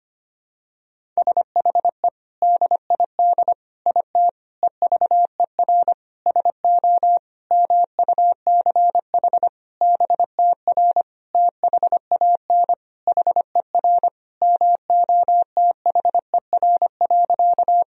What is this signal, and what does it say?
Morse code, heard as a keyed tone, about 25 wpm, 700 Hz. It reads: SHE DID IT EVER SO MUCH BTR THAN HER MOTHER.
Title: Sentences from Top 400 Words and Abbreviations 25wpm